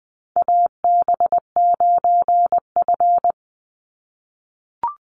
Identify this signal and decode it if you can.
Morse code, heard as a keyed tone, about 20 wpm, 700 Hz. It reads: AB9F
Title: USA Call Signs 20wpm